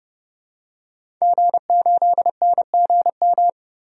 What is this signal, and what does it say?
Morse code, heard as a keyed tone, about 30 wpm, 700 Hz. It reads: G8NGM
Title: DXCC-100 Call Signs - 5 Characters - 30wpm